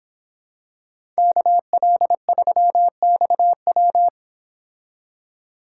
Morse code, heard as a keyed tone, about 25 wpm, 700 Hz. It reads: KL3XW
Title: Speed Racing - USA Call Signs 22wpm